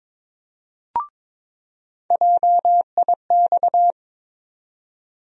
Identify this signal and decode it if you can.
Morse code, heard as a keyed tone, about 21 wpm, 700 Hz. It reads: JIX
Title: USA Call Sign Postfixes - 3 Letters 22wpm